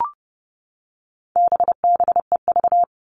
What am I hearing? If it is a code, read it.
Morse code, heard as a keyed tone, about 30 wpm, 700 Hz. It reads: BBEV
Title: All Cut Numbers - 4-Digit 30wpm